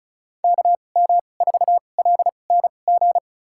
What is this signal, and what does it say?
Morse code, heard as a keyed tone, about 36 wpm, spaced as for 25 wpm, 700 Hz. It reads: KM4LNG
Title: USA Call Signs - 6 Characters - 2x Character Spacing - 35wpm